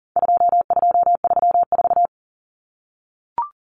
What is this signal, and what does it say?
Morse code, heard as a keyed tone, about 40 wpm, 700 Hz. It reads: 2234
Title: Common Cut Numbers - 4-Digit 40wpm